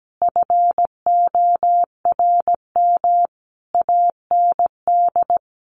Morse code, heard as a keyed tone, about 17 wpm, 700 Hz. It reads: FORM AND
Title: Sentences from Top 1200 Words 17wpm